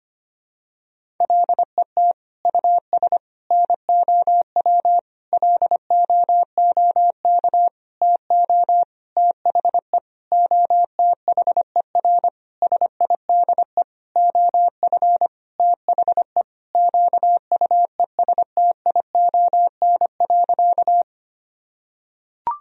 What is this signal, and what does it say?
Morse code, heard as a keyed tone, about 25 wpm, 700 Hz. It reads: LET US NOW LOOK TO THE OTHER SIDE OF THE QUESTION.
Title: Sentences from Top 1000 Words - Review Part 1 - 25wpm